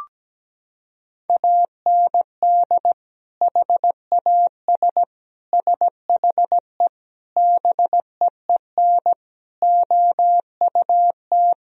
Morse code, heard as a keyed tone, about 17 wpm, 700 Hz. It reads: AND HAS SHE BEEN OUT
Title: Sentences from Top 1000 Words - Review Part 1 - 17wpm